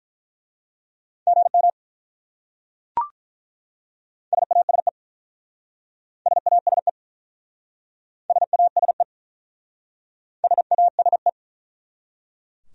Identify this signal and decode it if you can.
Morse code, heard as a keyed tone, about 46 wpm, 700 Hz. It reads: GM SASE SASE SASE SASE